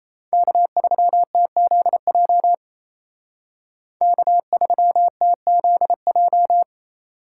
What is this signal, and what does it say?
Morse code, heard as a keyed tone, about 34 wpm, 700 Hz. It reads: K3TZJ K3TZJ
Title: Speed Racing - USA 1x3 Call Signs 28wpm